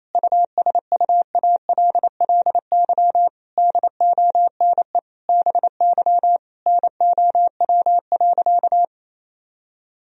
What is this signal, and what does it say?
Morse code, heard as a keyed tone, about 28 wpm, 700 Hz. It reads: USUALLY DONE BY NOW.
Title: Sentences from Top 1000 Words - Review Part 3 - 28wpm